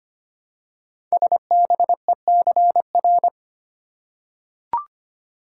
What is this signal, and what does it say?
Morse code, heard as a keyed tone, about 25 wpm, 700 Hz. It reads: SBECR
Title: Code Groups - 5 Letters - 25wpm